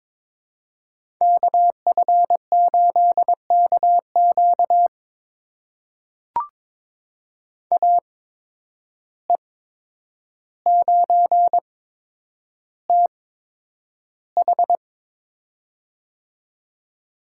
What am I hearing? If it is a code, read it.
Morse code, heard as a keyed tone, about 22 wpm, spaced as for 9 wpm, 700 Hz. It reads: KF8KQ AE9TH
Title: USA Call Signs - 5 Characters - 8x Character Spacing - 22wpm